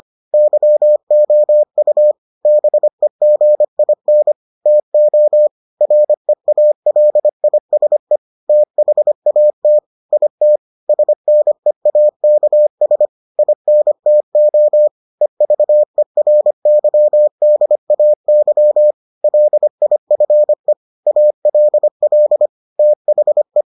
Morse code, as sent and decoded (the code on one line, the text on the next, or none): -.-- --- ..- / -... --. .. -. / - --- / .-. .- .-.. .. ... / - .... .- - / .. - / ... -. .- -.- ... / .. -. - --- / . ...- . .-. -.-- -.. .- -.-- / .-.. .. ..-. / .- .-.. .-.. / - .... .
YOU BEGIN TO REALISE THAT IT SNEAKS INTO EVERYDAY LIFE ALL THE